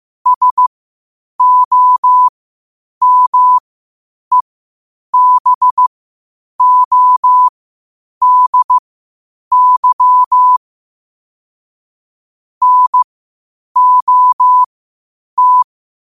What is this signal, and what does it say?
Morse code, heard as a keyed tone, about 15 wpm, spaced as for 8 wpm, 1000 Hz. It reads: SOMEBODY NOT
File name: qotdfiles / 8wpm